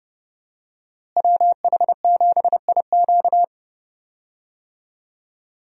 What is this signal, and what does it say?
Morse code, heard as a keyed tone, about 31 wpm, 700 Hz. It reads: WH7IQ
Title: USA Call Signs - 5 Characters - Repeated 3x - 30wpm